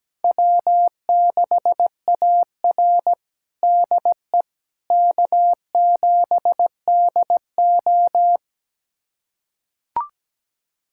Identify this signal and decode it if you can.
Morse code, heard as a keyed tone, about 17 wpm, 700 Hz. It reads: W6AR DE K7DO
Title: USA Callsign DE Callsign - 4 Characters - 17wpm